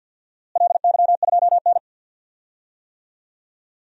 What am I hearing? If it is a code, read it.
Morse code, heard as a keyed tone, about 50 wpm, 700 Hz. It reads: LY2N
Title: DXCC-100 Call Signs - 4 Characters - Repeated 2x - 50wpm